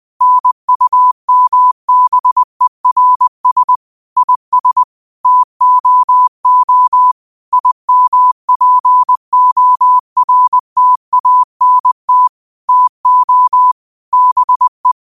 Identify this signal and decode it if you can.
Morse code, heard as a keyed tone, about 20 wpm, 1000 Hz. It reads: NUMBERS IS TOO IMPORTANT TO BE